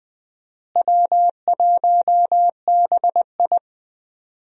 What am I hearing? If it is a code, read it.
Morse code, heard as a keyed tone, about 20 wpm, 700 Hz. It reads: W1BI
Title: USA Call Signs - 4 Characters - Repeated 3x - 20wpm